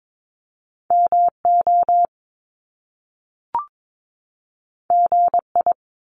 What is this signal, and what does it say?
Morse code, heard as a keyed tone, about 22 wpm, 700 Hz. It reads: MO GI
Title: Top Two Letter Combinations in Words 22wpm